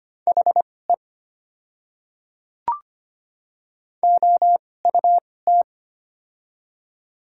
Morse code, heard as a keyed tone, about 25 wpm, spaced as for 18 wpm, 700 Hz. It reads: HE OUT
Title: 2x Character Spacing - Top 100 Words - 25wpm